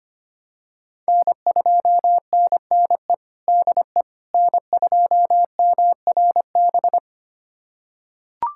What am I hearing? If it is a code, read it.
Morse code, heard as a keyed tone, about 25 wpm, 700 Hz. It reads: N2NNE DE N2MRB
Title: USA Callsign DE Callsign - 5 Characters - 25wpm